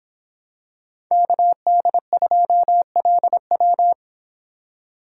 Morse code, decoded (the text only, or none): KD2LW